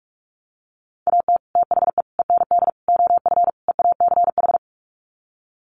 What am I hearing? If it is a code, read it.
Morse code, heard as a keyed tone, about 44 wpm, 700 Hz. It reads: AT THE END OF EACH
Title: Sets of 5 Words Encore 45wpm